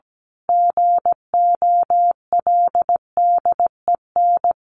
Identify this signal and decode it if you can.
Morse code, heard as a keyed tone, about 17 wpm, 700 Hz. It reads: GOLDEN
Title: A Sound Of Thunder 17wpm